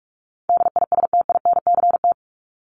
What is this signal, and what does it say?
Morse code, heard as a keyed tone, about 44 wpm, 700 Hz. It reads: DISTINCT
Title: Top 1501-1750 Words 45wpm